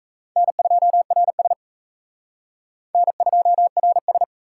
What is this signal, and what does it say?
Morse code, heard as a keyed tone, about 35 wpm, 700 Hz. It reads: N2RS N2RS
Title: Speed Racing - USA 1x2 Call Signs 28wpm